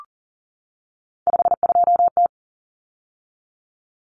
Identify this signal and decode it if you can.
Morse code, heard as a keyed tone, about 40 wpm, 700 Hz. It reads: 52T
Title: Common Cut Numbers - 3-Digit 40wpm